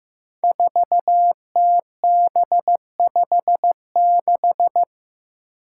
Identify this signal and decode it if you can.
Morse code, heard as a keyed tone, about 15 wpm, 700 Hz. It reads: 4TB56
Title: Five Challenging Letter-Number Combinations 15wpm